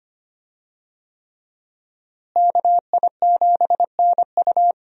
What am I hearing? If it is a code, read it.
Morse code, heard as a keyed tone, about 25 wpm, 700 Hz. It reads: KI7NU
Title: USA Call Signs - 5 Characters - Repeated 3x - 25wpm